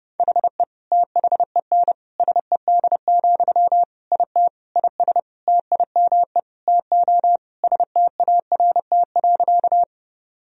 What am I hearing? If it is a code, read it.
Morse code, heard as a keyed tone, about 30 wpm, 700 Hz. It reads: HE THEN SED, IT IS TIME TO START.
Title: Sentences from Top 300 Words and Abbreviations 30wpm